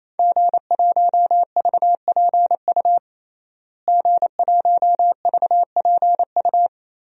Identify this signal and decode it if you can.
Morse code, heard as a keyed tone, about 28 wpm, 700 Hz. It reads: G1VPU G1VPU
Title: DXCC-100 Call Signs - 5 Characters - Repeated 3x - 28wpm